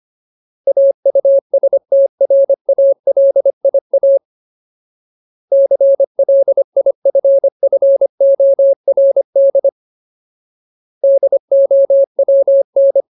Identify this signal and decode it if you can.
Morse code, heard as a keyed tone, about 25 wpm, 550 Hz. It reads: AUSTRALIA CLIFFORD DOWN
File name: Ultimate Word List / 25WPM Ultimate